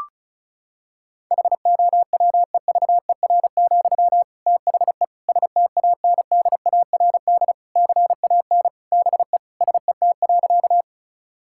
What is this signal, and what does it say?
Morse code, heard as a keyed tone, about 36 wpm, 700 Hz. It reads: HOWEVER, THE STANDARD CAN BE SET.
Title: Sentences from Top 1000 Words - Review Part 6 - 35wpm